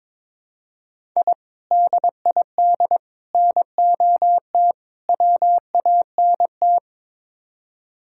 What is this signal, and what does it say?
Morse code, heard as a keyed tone, about 22 wpm, 700 Hz. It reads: I DID NOT WANT
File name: practice / 22wpm / Build A Phrase - 5-Word Phrases - 22wpm